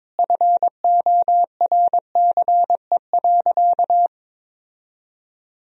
Morse code, heard as a keyed tone, about 22 wpm, 700 Hz. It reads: FORCE.
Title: Sentences from Top 1000 Words - Review Part 3 - 22wpm